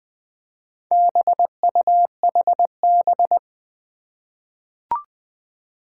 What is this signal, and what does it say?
Morse code, heard as a keyed tone, about 21 wpm, 700 Hz. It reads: BUHB
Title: Four Challenging Letter-Number Combinations 20wpm